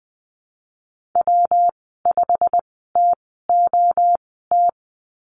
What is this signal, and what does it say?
Morse code, heard as a keyed tone, about 20 wpm, spaced as for 15 wpm, 700 Hz. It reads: W5TOT
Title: 122 Lesson 20g - Number 3 - Call Signs 20wpm